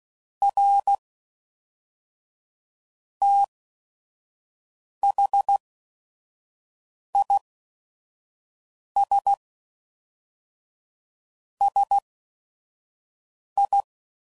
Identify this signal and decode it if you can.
Morse code, heard as a keyed tone, about 16 wpm, spaced as for 5 wpm, 800 Hz. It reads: R THIS SI
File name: CW Code Groups @5WPM